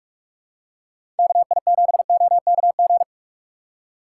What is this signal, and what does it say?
Morse code, heard as a keyed tone, about 46 wpm, 700 Hz. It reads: KI7OKG
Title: USA Call Signs - 6 Characters - 2x Character Spacing - 45wpm